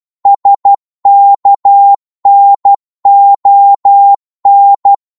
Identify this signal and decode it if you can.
Morse code, heard as a keyed tone, about 12 wpm, 800 Hz. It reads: SKNON
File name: Morse / ltr-12wpm